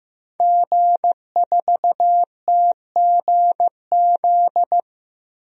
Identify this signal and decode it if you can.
Morse code, heard as a keyed tone, about 15 wpm, 700 Hz. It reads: G4TGZ
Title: DXCC-100 Call Signs - 5 Characters - Repeated 2x - 15wpm